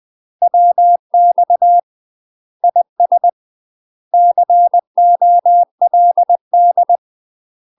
Morse code, heard as a keyed tone, about 20 wpm, 700 Hz. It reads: WX IS COLD
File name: UK MASTER QSOs TRAINING / 20WPM / UK MASTER QSOs 0063